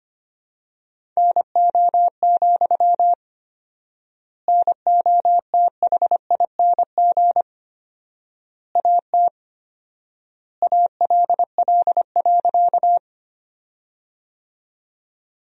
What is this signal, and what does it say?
Morse code, heard as a keyed tone, about 25 wpm, 700 Hz. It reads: NO, NOTHING AT ALL.
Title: 4x Word-Spacing - Sentences from Top 500 Words - 25wpm